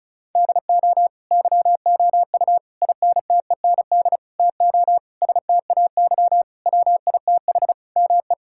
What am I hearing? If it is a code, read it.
Morse code, heard as a keyed tone, about 36 wpm, 700 Hz. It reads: DO YOU INTEND TO STAY WITH ME
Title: Sentences from Top 1600 Words 35wpm